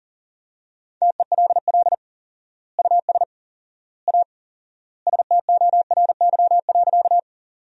Morse code, heard as a keyed tone, about 40 wpm, 700 Hz. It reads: TELL US A STORY.
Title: 4x Word-Spacing - Sentences from Top 300 Words - 40wpm